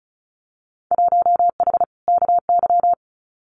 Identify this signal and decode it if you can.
Morse code, heard as a keyed tone, about 36 wpm, 700 Hz. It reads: 1H KY
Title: Winter Field Day Exchanges 35wpm